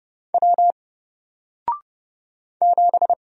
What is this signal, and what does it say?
Morse code, heard as a keyed tone, about 30 wpm, 700 Hz. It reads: W 7
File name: Single-Letter-Number-30wpm